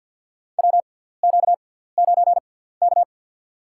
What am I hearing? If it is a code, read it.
Morse code, heard as a keyed tone, about 53 wpm, spaced as for 17 wpm, 700 Hz. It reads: WQ9K
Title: USA Call Signs - 4 Characters - 6x Character Spacing - 50wpm